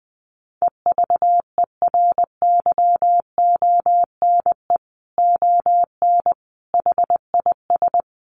Morse code, heard as a keyed tone, about 20 wpm, 700 Hz. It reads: EVERYONE ON HIS
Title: A Sound Of Thunder 20wpm